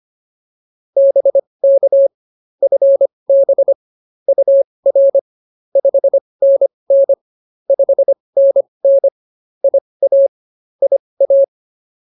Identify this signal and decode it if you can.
Morse code, heard as a keyed tone, about 24 wpm, spaced as for 20 wpm, 550 Hz. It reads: BK FB UR 5NN 5NN IA IA